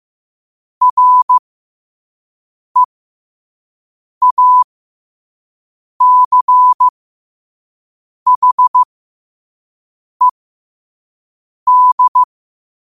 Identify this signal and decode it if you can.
Morse code, heard as a keyed tone, about 15 wpm, spaced as for 5 wpm, 1000 Hz. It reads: REACHED
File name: qotdfiles / 5wpm